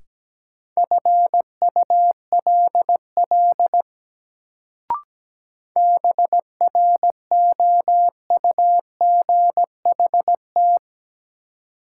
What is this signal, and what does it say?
Morse code, heard as a keyed tone, about 17 wpm, 700 Hz. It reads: FULL BROUGHT
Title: Sentences from Top 500 Words - New Word Drill - 17wpm